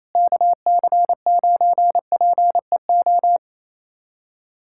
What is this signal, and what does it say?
Morse code, heard as a keyed tone, about 28 wpm, 700 Hz. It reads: KC9PEO
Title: USA Callsign DE Callsign 28wpm